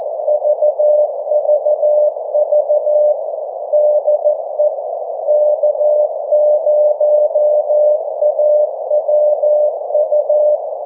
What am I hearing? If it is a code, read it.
Morse code, heard as a keyed tone, about 14 wpm, 600 Hz. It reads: VVV DE K0AWU
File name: 1899khz / k0awu 1742ut-24jan21